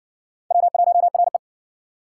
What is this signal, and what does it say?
Morse code, heard as a keyed tone, about 60 wpm, 700 Hz. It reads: W1RE